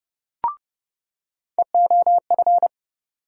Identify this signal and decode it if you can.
Morse code, heard as a keyed tone, about 30 wpm, 700 Hz. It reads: EOF
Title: USA Call Sign Postfixes - 3 Letters 30wpm